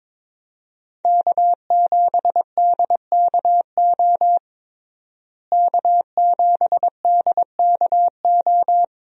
Morse code, heard as a keyed tone, about 22 wpm, 700 Hz. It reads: K7DKO K7DKO
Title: USA Call Signs - 5 Characters - Repeated 2x - 22wpm